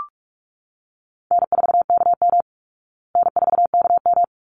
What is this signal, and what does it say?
Morse code, heard as a keyed tone, about 44 wpm, 700 Hz. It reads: N4KM N4KM
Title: Easy Call Signs Repeated 2x - 45wpm